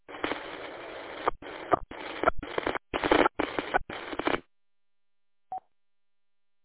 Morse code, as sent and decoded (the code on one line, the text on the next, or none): none